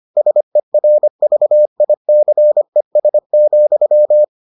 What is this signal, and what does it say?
Morse code, heard as a keyed tone, about 25 wpm, 600 Hz. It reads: SERVICES,